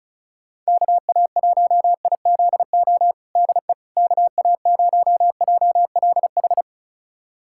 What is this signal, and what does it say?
Morse code, heard as a keyed tone, about 36 wpm, 700 Hz. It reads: KA1IZO DE KA0JLH
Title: USA Callsign DE Callsign - 6 Characters - 35wpm